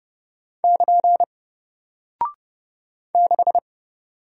Morse code, heard as a keyed tone, about 30 wpm, 700 Hz. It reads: <KN> 6
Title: Single Character (Letter:Number:Punctuation:Prosign) Alt 30wpm